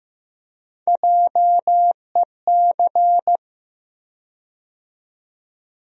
Morse code, heard as a keyed tone, about 15 wpm, 700 Hz. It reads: JEC
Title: USA Call Sign Postfixes - 3 Letters 15wpm